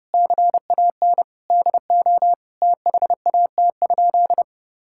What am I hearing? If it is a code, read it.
Morse code, heard as a keyed tone, about 30 wpm, 700 Hz. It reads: CAN DO THAT?
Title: Sentences from Top 1000 Words - Review Part 6 - 30wpm